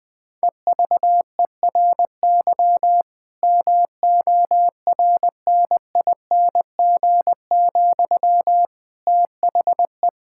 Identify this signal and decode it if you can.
Morse code, heard as a keyed tone, about 20 wpm, 700 Hz. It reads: EVERY MORNING, THE
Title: Difficult Sentences with Hints - 20wpm